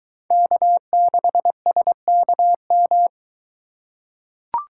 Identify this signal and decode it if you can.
Morse code, heard as a keyed tone, about 22 wpm, 700 Hz. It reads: K6SKM